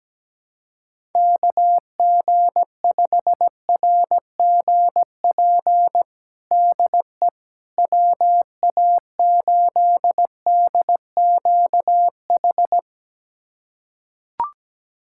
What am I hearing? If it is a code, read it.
Morse code, heard as a keyed tone, about 17 wpm, 700 Hz. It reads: KG5RGP DE WA8DQH
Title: USA Callsign DE Callsign 17wpm